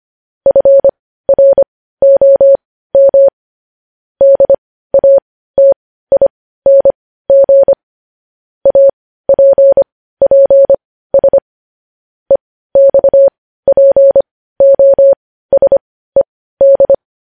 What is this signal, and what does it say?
Morse code, heard as a keyed tone, about 25 wpm, spaced as for 15 wpm, 550 Hz. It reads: FROM DATING APPS EXPOSED
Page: news-headlines-in-morse-code-at-15-wpm